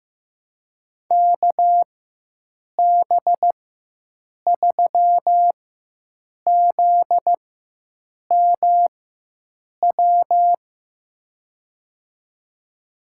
Morse code, heard as a keyed tone, about 15 wpm, spaced as for 7 wpm, 700 Hz. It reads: KB3ZMW
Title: USA Call Signs - 6 Characters - 4x Character Spacing - 15wpm